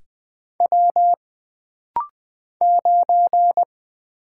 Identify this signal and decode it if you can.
Morse code, heard as a keyed tone, about 20 wpm, 700 Hz. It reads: W 9
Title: Single Letter-Number - Rapid-Fire 20wpm